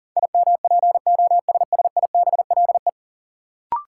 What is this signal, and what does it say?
Morse code, heard as a keyed tone, about 37 wpm, 700 Hz. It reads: IMPOSSIBLE
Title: Sentences from Top 1100 Words - New Word Drill Alt - 40wpm